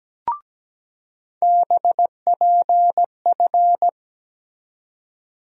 Morse code, heard as a keyed tone, about 17 wpm, 700 Hz. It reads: BPF
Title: USA Call Sign Postfixes - 3 Letters 17wpm